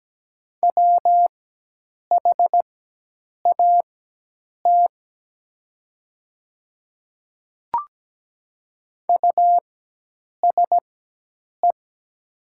Morse code, heard as a keyed tone, about 17 wpm, spaced as for 8 wpm, 700 Hz. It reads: WHAT USE
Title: 4x Character Spacing - Top 100 Words - 17wpm